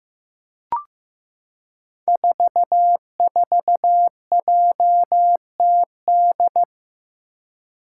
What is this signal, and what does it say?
Morse code, heard as a keyed tone, about 15 wpm, 700 Hz. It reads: 44JTD
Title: Five Challenging Letter-Number Combinations 15wpm